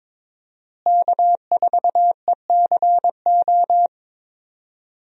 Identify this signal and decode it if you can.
Morse code, heard as a keyed tone, about 22 wpm, 700 Hz. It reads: K4ECO
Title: USA Call Signs - 5 Characters - Repeated 3x - 22wpm